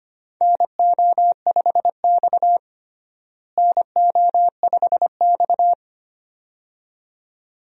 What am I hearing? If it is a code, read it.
Morse code, heard as a keyed tone, about 25 wpm, 700 Hz. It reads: NO5X NO5X
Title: USA Call Signs - 4 Characters - Repeated 2x - 25wpm